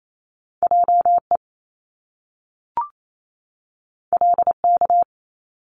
Morse code, heard as a keyed tone, about 28 wpm, 700 Hz. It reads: JE LK
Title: Top Two Letter Combinations in Words 28wpm